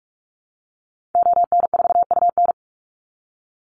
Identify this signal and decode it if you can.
Morse code, heard as a keyed tone, about 44 wpm, 700 Hz. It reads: ON4UN
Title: Call Signs - 45wpm